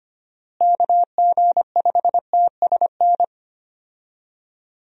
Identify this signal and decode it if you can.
Morse code, heard as a keyed tone, about 25 wpm, 700 Hz. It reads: KG5TSN
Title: USA Call Signs - 6 Characters - Repeated 2x - 25wpm